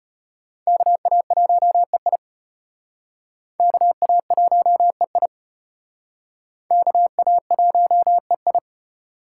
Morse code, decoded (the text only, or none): KA1EI KA1EI KA1EI